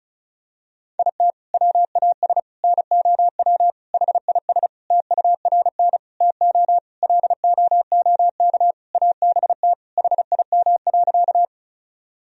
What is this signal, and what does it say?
Morse code, heard as a keyed tone, about 34 wpm, 700 Hz. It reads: IT WAS NOW HIS TURN TO LOOK ABT HIM.
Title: Sentences from Top 200 Words and Abbreviations 35wpm